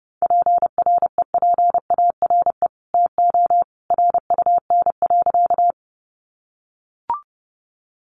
Morse code, heard as a keyed tone, about 30 wpm, 700 Hz. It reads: PREPARE TO RUN.
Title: Sentences from Top 900 Words 30wpm